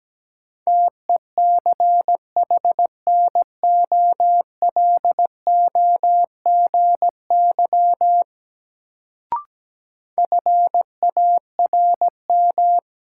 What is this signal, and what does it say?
Morse code, heard as a keyed tone, about 17 wpm, 700 Hz. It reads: TECHNOLOGY FARM
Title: Sentences from Top 1200 Words - New Word Drill Alt - 17wpm